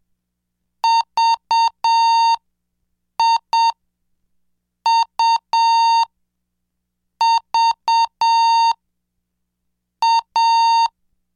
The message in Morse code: VIUVA